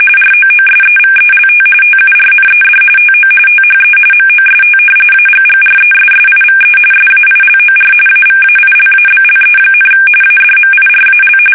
RTTY, about 45.45 baud. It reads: ROOSEVELT TURNED TO THE MOST AGGRESSIVE SAILOR AVAILABLE, ADMIRAL
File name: fsk850